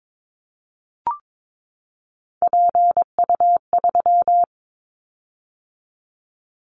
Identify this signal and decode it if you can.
Morse code, heard as a keyed tone, about 22 wpm, 700 Hz. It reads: PU3
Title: Callsign Prefixes for Top 100 DXCC - 22wpm